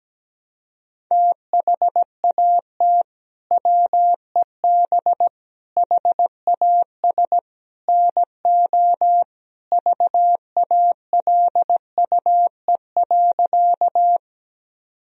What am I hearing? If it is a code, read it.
Morse code, heard as a keyed tone, about 17 wpm, 700 Hz. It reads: THAT WEB HAS NO VALUE.
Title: Sentences from Top 1000 Words - Review Part 4 - 17wpm